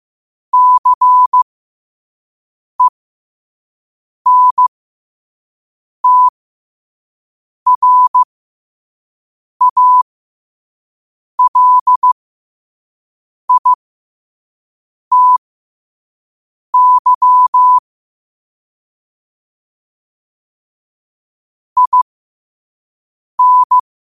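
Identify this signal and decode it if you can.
Morse code, heard as a keyed tone, about 15 wpm, spaced as for 5 wpm, 1000 Hz. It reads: CENTRALITY IN
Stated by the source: Morse sent at 5 wpm.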